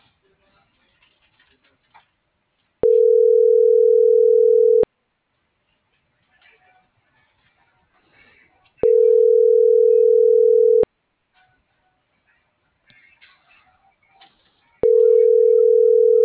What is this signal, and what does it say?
Ringback tone (North America), 440 + 480 Hz.
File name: CENTRAL MULTIMODAL / out cecisandoval